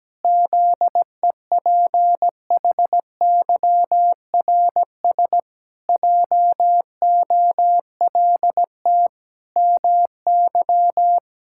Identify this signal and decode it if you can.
Morse code, heard as a keyed tone, about 17 wpm, 700 Hz. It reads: ZEPHYRS JOLT MY
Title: Pangram Challenge v2 - 17wpm